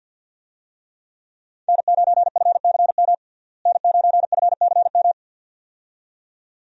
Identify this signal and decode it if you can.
Morse code, heard as a keyed tone, about 49 wpm, 700 Hz. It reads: N9FCM N9FCM
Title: USA Call Signs - 5 Characters - Repeated 2x - 50wpm